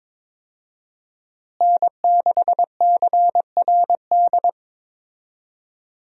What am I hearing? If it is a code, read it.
Morse code, heard as a keyed tone, about 22 wpm, 700 Hz. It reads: N6CRD
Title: USA Call Signs - 5 Characters - Repeated 3x - 22wpm